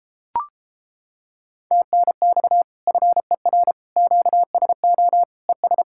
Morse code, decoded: TNX FER QSO ES

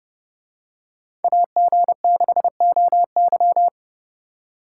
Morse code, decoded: AG6OY

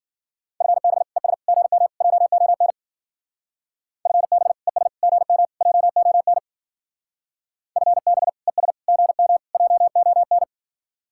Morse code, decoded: FB ES GM JON FB ES GM JON FB ES GM JON

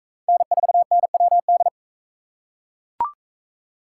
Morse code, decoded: N4NWD

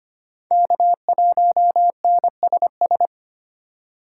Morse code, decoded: K1NSS

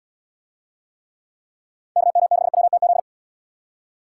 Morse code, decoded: KM6WTZ

